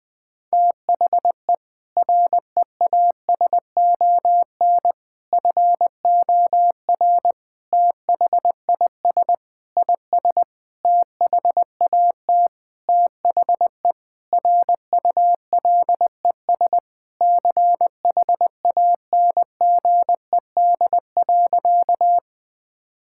Morse code, heard as a keyed tone, about 20 wpm, 700 Hz. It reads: THE REASON FOR THIS IS THAT THE RULES CHANGED.
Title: Sentences with 7-Word Phrases - Spoken Beginning - 20wpm